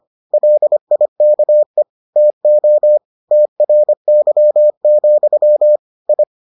Morse code, heard as a keyed tone, about 25 wpm, 600 Hz. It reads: LIKE TO TRY, I